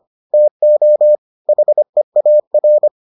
Morse code, heard as a keyed tone, about 25 wpm, 600 Hz. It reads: TO HEAR